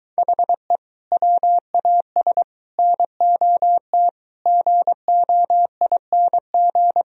Morse code, heard as a keyed tone, about 24 wpm, 700 Hz. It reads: HE WAS NOT GOING